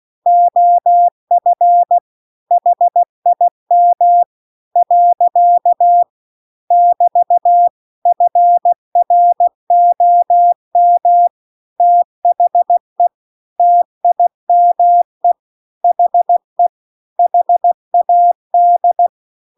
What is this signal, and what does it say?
Morse code, heard as a keyed tone, about 16 wpm, 700 Hz. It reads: OF HIM . <BT> FROM THE TIME HE HAD